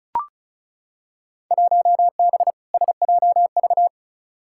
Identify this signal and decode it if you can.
Morse code, heard as a keyed tone, about 35 wpm, 700 Hz. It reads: 1B SJV
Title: Field Day Exchanges Encore - 35wpm